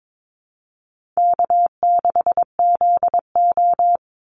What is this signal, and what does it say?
Morse code, heard as a keyed tone, about 22 wpm, 700 Hz. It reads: K6ZO
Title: Easy Call Signs Repeated 2x - 22wpm